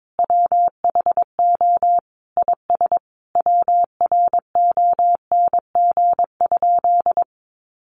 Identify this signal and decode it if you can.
Morse code, heard as a keyed tone, about 22 wpm, 700 Hz. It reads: WHO IS WRONG?
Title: Sentences from Top 900 Words 22wpm